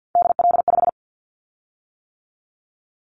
Morse code, heard as a keyed tone, about 50 wpm, 700 Hz. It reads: DL5
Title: Callsign Prefixes for Top 25 DXCC - 50wpm